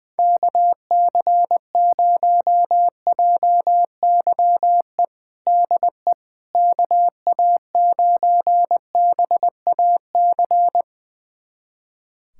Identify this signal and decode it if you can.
Morse code, heard as a keyed tone, about 20 wpm, 700 Hz. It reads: KC0JYE DE KA9BAC
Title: USA Callsign DE Callsign - 6 Characters - 20wpm